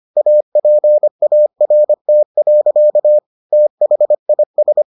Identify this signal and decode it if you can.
Morse code, heard as a keyed tone, about 25 wpm, 600 Hz. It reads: APART. THIS